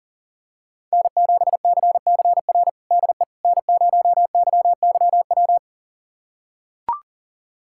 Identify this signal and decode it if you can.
Morse code, heard as a keyed tone, about 37 wpm, 700 Hz. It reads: N7CCR DE N0YYW
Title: USA Callsign DE Callsign - 5 Characters - 40wpm